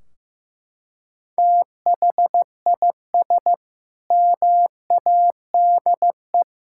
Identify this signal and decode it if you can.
Morse code, heard as a keyed tone, about 16 wpm, 700 Hz. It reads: THIS MADE
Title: Sentences from Top 1600 Words 15wpm